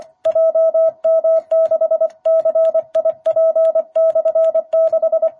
Morse code, heard as a keyed tone, about 25 wpm, 660 Hz. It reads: JM6CIP/6
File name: jh8cbh / 2021cw